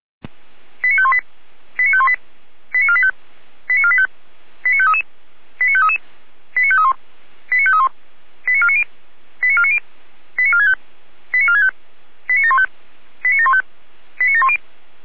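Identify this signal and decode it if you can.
Five-tone selective call (ZVEI), heard as quick runs of five tones, tones of about 70 ms. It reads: 89518 89518 89485 89485 89533 89533 89531 89531 89490 89490 89465 89465 89715 89715 89710
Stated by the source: ZVEI five-tone call 89010 received on an SDR, from the feuerwehr-orlamuende.org (Orlamünde fire brigade) audio server.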